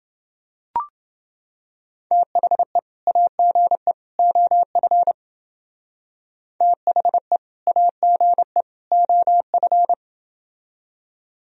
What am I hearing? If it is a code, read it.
Morse code, heard as a keyed tone, about 28 wpm, 700 Hz. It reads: THE AGE OF THE AGE OF